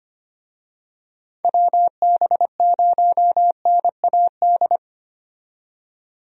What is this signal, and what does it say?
Morse code, heard as a keyed tone, about 25 wpm, 700 Hz. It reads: WB0NAD
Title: USA Call Signs - 6 Characters - 25wpm